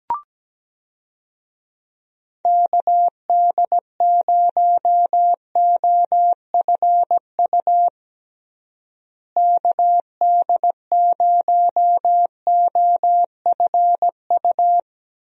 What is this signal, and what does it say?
Morse code, heard as a keyed tone, about 17 wpm, 700 Hz. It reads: KD0OFU KD0OFU
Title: USA Call Signs - 6 Characters - Repeated 2x - 17wpm